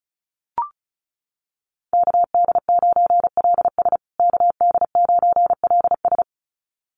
Morse code, heard as a keyed tone, about 34 wpm, 700 Hz. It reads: KD9LS KD9LS
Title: USA Call Signs Repeated 2x 35wpm